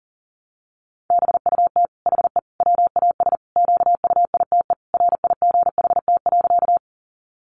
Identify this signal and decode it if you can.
Morse code, heard as a keyed tone, about 40 wpm, 700 Hz. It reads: BUT HE WAS QUITE RIGHT.
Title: Sentences from Top 700 Words 40wpm